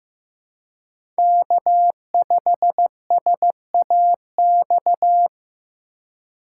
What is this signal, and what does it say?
Morse code, heard as a keyed tone, about 15 wpm, 700 Hz. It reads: K5SAX
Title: USA Call Signs - 5 Characters - 2x Character Spacing - 15wpm